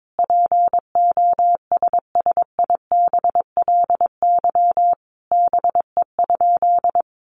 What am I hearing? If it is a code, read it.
Morse code, heard as a keyed tone, about 22 wpm, 700 Hz. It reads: POSSIBLY BE?
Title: Sentences from Top 1300 Words 22wpm